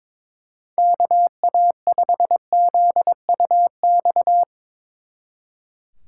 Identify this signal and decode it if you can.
Morse code, heard as a keyed tone, about 23 wpm, 700 Hz. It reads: KA5ZUX
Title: Speed Racing - USA 2x3 Call Signs 22wpm